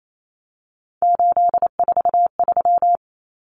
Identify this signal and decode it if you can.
Morse code, heard as a keyed tone, about 28 wpm, 700 Hz. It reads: 843
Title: Common Cut Numbers - 3-Digit 28wpm